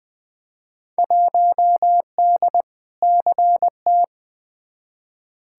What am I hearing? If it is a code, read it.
Morse code, heard as a keyed tone, about 21 wpm, 700 Hz. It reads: 1D CT
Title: Field Day Exchanges - 20wpm